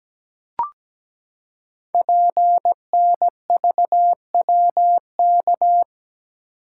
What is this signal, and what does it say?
Morse code, heard as a keyed tone, about 17 wpm, 700 Hz. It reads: PNVWK
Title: Code Groups - 5 Letters - 17wpm